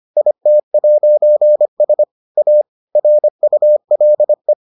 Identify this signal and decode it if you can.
Morse code, heard as a keyed tone, about 25 wpm, 600 Hz. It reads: IT'S A RULE